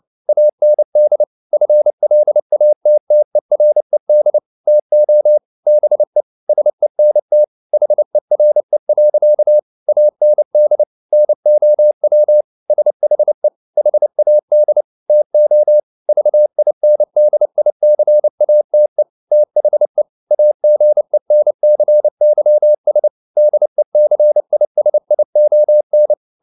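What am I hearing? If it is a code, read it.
Morse code, heard as a keyed tone, about 29 wpm, 600 Hz. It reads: AND FLATTERED TO BE SENT HERE. AND NOW SHE HAD TO VINDICATE THE AGENCYS DECISION
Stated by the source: Morse code at 29 wpm, sending Ken Follett's novel Never in English, from CW med Gnister.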